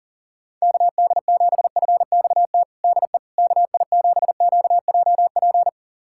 Morse code, heard as a keyed tone, about 40 wpm, 700 Hz. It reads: KD7FXT DE KI7QJP
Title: USA Callsign DE Callsign - 6 Characters - 40wpm